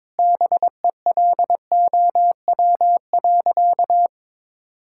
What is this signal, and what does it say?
Morse code, heard as a keyed tone, about 22 wpm, 700 Hz. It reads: BELOW.
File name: Sentences with 9-Word Phrases - 22wpm